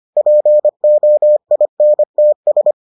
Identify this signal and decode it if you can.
Morse code, heard as a keyed tone, about 25 wpm, 600 Hz. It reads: POINTS